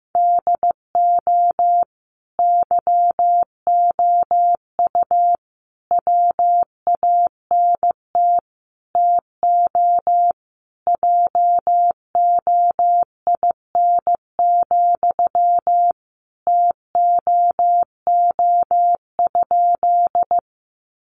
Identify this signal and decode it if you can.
Morse code, heard as a keyed tone, about 15 wpm, 700 Hz. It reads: DO YOU WANT TO JOIN, TOO?
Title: Sentences from Top 1400 Words 15wpm